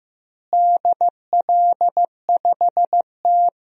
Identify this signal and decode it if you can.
Morse code, heard as a keyed tone, about 16 wpm, 700 Hz. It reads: DL5T
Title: DXCC-100 Call Signs - 4 Characters - Repeated 3x - 15wpm